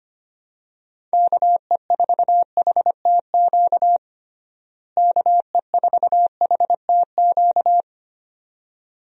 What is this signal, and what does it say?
Morse code, heard as a keyed tone, about 25 wpm, 700 Hz. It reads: KE4HTQ KE4HTQ
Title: USA Call Signs - 6 Characters - Repeated 2x - 25wpm